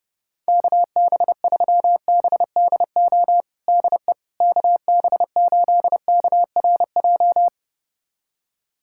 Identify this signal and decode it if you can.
Morse code, heard as a keyed tone, about 30 wpm, 700 Hz. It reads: KB3BDO DE KB8KRJ
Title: USA Callsign DE Callsign 30wpm